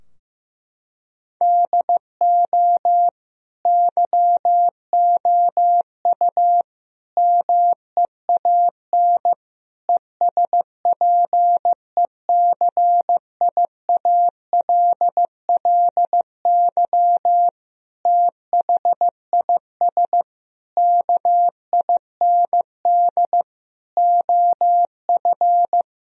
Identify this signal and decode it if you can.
Morse code, heard as a keyed tone, about 15 wpm, 700 Hz. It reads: DO YOU MEAN ESPECIALLY THIS KIND OF
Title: Sentences from Top 1000 Words - Review Part 4 - 15wpm